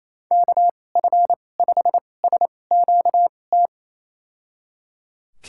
Morse code, heard as a keyed tone, about 27 wpm, spaced as for 20 wpm, 700 Hz. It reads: KF5SQT
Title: USA Call Signs - 6 Characters - 2x Character Spacing - 28wpm